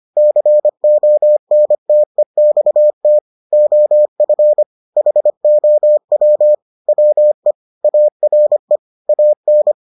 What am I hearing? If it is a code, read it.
Morse code, heard as a keyed tone, about 25 wpm, 600 Hz. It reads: CONTEXT OF HOW WE ARE AN